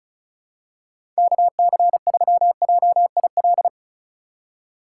Morse code, heard as a keyed tone, about 36 wpm, 700 Hz. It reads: KC3JIL
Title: USA Call Signs - 6 Characters - 3x Character Spacing - 35wpm